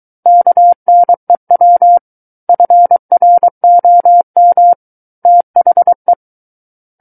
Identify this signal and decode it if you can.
Morse code, heard as a keyed tone, about 23 wpm, 700 Hz. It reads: KNEW FROM THE
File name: Book One / 14-Chapter 14 - 23 WPM